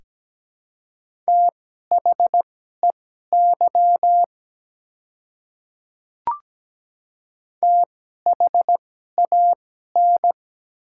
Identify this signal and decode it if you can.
Morse code, heard as a keyed tone, about 17 wpm, spaced as for 12 wpm, 700 Hz. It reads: THEY THAN
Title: 2x Character Spacing - Top 100 Words - 17wpm